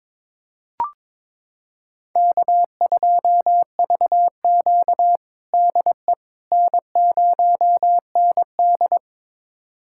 Morse code, heard as a keyed tone, about 22 wpm, 700 Hz. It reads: K2VQ DE N0ND
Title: USA Callsign DE Callsign - 4 Characters - 22wpm